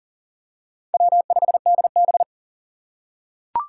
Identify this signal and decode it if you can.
Morse code, heard as a keyed tone, about 39 wpm, 700 Hz. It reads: W5DB
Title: USA Call Signs - 4 Characters - 2x Character Spacing - 40wpm